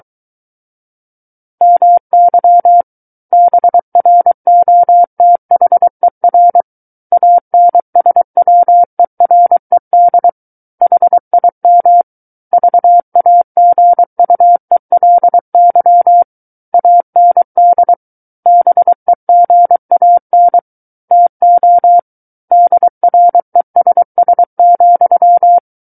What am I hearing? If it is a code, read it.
Morse code, heard as a keyed tone, about 23 wpm, 700 Hz. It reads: MY BROTHER ANSWERED HIM VAGUELY AND BEGAN TO DRESS,
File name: Book One / 14-Chapter 14 - 23 WPM